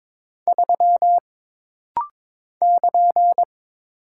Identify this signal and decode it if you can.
Morse code, heard as a keyed tone, about 22 wpm, 700 Hz. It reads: 3 <KN>
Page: This rapid-fire timing is geared to push your ICR (Instant Character Recognition) skills.